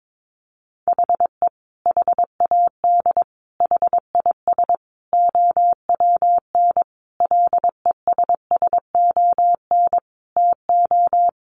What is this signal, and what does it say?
Morse code, heard as a keyed tone, about 22 wpm, 700 Hz. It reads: HE HAD HIS OWN LESSON TO
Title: Sentences from Top 1200 Words 22wpm